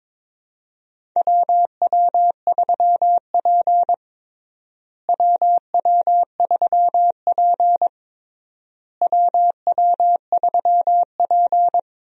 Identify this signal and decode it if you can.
Morse code, heard as a keyed tone, about 22 wpm, 700 Hz. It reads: WW3P WW3P WW3P
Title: USA Call Signs - 4 Characters - Repeated 3x - 22wpm